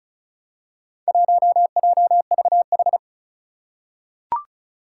Morse code, heard as a keyed tone, about 36 wpm, 700 Hz. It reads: 1JVH
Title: Four Challenging Letter-Number Combinations 35wpm